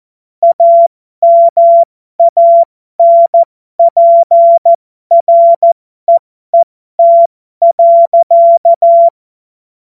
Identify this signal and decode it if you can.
Morse code, heard as a keyed tone, about 14 wpm, spaced as for 12 wpm, 680 Hz. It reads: AMANPREET.